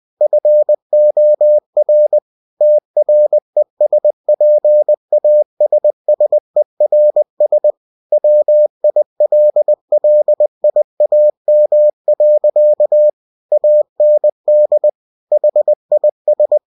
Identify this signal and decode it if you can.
Morse code, heard as a keyed tone, about 20 wpm, 600 Hz. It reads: FOR TRESPASSERS WILLIAM. AND HIS
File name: cw-books / winnie-the-pooh / winnie-20wpm-page-0020